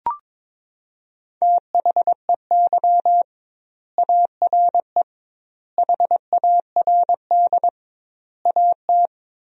Morse code, heard as a keyed tone, about 21 wpm, 700 Hz. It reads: THEY ARE HARD AT
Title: 2x Word-Spacing - Sentences from Top 300 Words - 22wpm